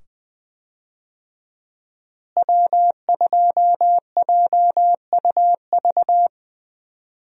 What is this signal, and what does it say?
Morse code, heard as a keyed tone, about 21 wpm, 700 Hz. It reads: W2JUV
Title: USA Call Signs - 5 Characters - Repeated 2x - 20wpm